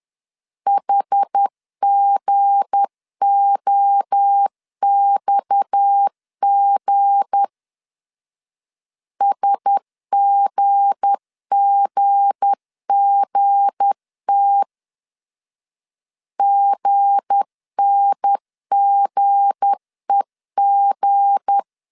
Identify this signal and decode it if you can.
Morse code, heard as a keyed tone, about 10 wpm, 800 Hz. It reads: HGOXG SGGGT GNGEG
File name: LEZIONI / 7 GF9Z / 7 G 10wpm 2501